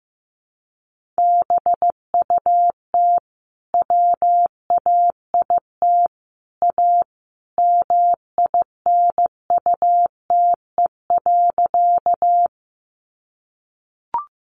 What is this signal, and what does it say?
Morse code, heard as a keyed tone, about 15 wpm, 700 Hz. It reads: BUT WAIT A MINUTE.
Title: Sentences from Top 500 Words 15wpm